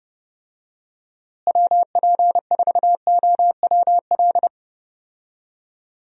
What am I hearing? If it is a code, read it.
Morse code, heard as a keyed tone, about 31 wpm, 700 Hz. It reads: WP4OWL
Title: USA Call Signs - 6 Characters - Repeated 3x - 30wpm